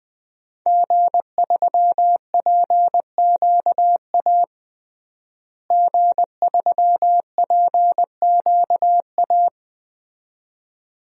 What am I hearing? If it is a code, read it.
Morse code, heard as a keyed tone, about 20 wpm, 700 Hz. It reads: G3PQA G3PQA
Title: DXCC-100 Call Signs - 5 Characters - Repeated 2x - 20wpm